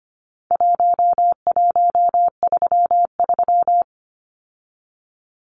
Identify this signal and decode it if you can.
Morse code, heard as a keyed tone, about 25 wpm, 700 Hz. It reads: 1133
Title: Numbers - Contest Style - 4-Digit 25wpm